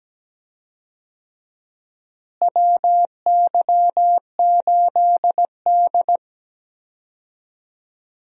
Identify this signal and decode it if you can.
Morse code, heard as a keyed tone, about 17 wpm, 700 Hz. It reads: WY8D
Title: USA Call Signs - 4 Characters - Repeated 3x - 17wpm